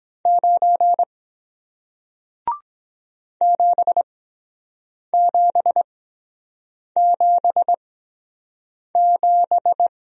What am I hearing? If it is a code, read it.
Morse code, heard as a keyed tone, about 23 wpm, spaced as for 7 wpm, 700 Hz. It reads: 9 7777